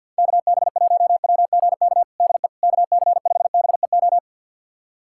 Morse code, heard as a keyed tone, about 52 wpm, 690 Hz. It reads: KB1WGK DE KC5BEO
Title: USA Callsign DE Callsign - 6 Characters - 50wpm